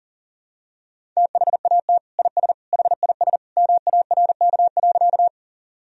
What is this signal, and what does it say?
Morse code, heard as a keyed tone, about 40 wpm, 700 Hz. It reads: THAT IS HIS MARK.
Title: Sentences from Top 1000 Words - Review Part 1 - 40wpm